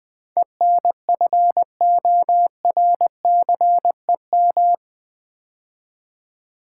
Morse code, heard as a keyed tone, about 20 wpm, 700 Hz. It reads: ENFORCEM
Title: Build A Word - 4 Letters At A Time - 11-Letter Words - 20wpm